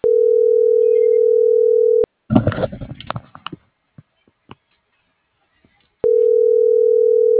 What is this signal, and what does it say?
Ringback tone (North America), 440 + 480 Hz.